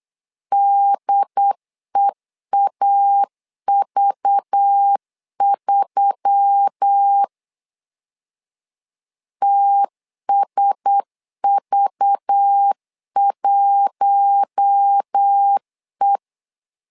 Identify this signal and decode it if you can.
Morse code, heard as a keyed tone, about 8 wpm, 800 Hz. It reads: DEAV3 TSV1E
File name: LEZIONI / 5 LBVC / 5 LBV 8wpm 2837